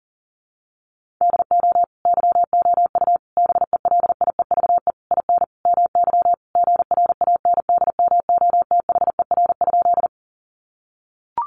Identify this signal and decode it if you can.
Morse code, heard as a keyed tone, about 40 wpm, 700 Hz. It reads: DO YOU BELIEVE IN MY GRANDMOTHER?
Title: Sentences from Top 1200 Words 40wpm